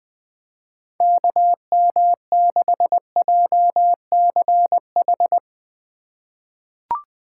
Morse code, heard as a keyed tone, about 20 wpm, 700 Hz. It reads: KM6JCH
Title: USA Call Signs - 6 Characters - 4x Character Spacing - 20wpm